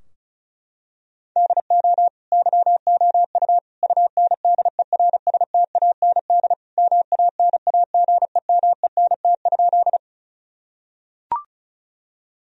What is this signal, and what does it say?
Morse code, heard as a keyed tone, about 36 wpm, 700 Hz. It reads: DO YOU UNDERSTAND MANAGEMENT?
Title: Sentences from Top 1000 Words - Review Part 4 - 35wpm